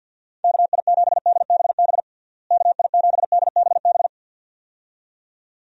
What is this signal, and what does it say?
Morse code, heard as a keyed tone, about 49 wpm, 700 Hz. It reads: KI7DBB KI7DBB
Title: USA Call Signs - 6 Characters - Repeated 2x - 50wpm